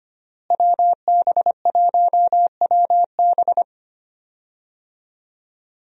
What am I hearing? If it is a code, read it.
Morse code, heard as a keyed tone, about 25 wpm, 700 Hz. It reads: WB1WB
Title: USA Call Signs - 5 Characters - Repeated 2x - 25wpm